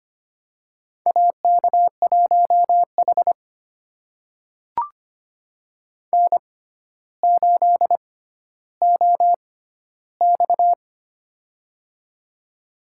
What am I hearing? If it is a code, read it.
Morse code, heard as a keyed tone, about 25 wpm, spaced as for 13 wpm, 700 Hz. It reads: AK1H N8OX
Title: USA Call Signs - 4 Characters - 6x Character Spacing - 25wpm